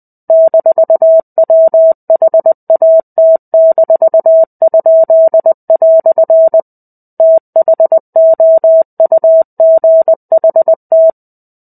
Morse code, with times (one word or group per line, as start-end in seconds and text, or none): 0.3-6.6 -WHAT-?"
7.2-11.1 THOUGHT